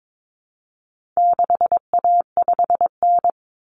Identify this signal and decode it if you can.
Morse code, heard as a keyed tone, about 22 wpm, 700 Hz. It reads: 6A5N
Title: Common Cut Numbers - 4-Digit 22wpm